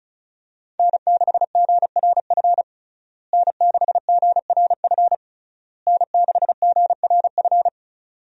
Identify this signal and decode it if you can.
Morse code, heard as a keyed tone, about 36 wpm, 700 Hz. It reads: N6GRF N6GRF N6GRF
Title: USA Call Signs - 5 Characters - Repeated 3x - 35wpm